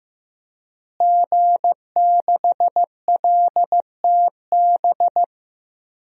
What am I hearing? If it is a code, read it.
Morse code, heard as a keyed tone, about 15 wpm, 700 Hz. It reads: G6LTB
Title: DXCC-100 Call Signs - 5 Characters - Repeated 2x - 15wpm